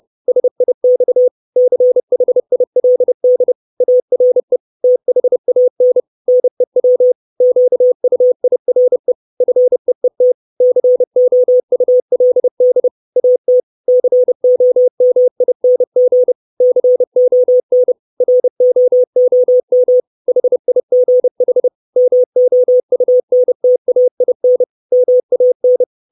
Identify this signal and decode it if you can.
Morse code, heard as a keyed tone, about 30 wpm, 500 Hz. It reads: SIX CHILD ARE THAN NEW QUIRE FEET COULD AT COMING CON ROOM HIGH MOUNTAIN MAN
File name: 632 Words 30WPM